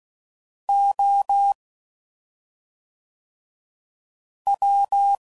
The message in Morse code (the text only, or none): O W